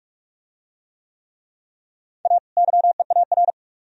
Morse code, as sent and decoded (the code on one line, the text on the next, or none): .- / -.-- . .- .-.
A YEAR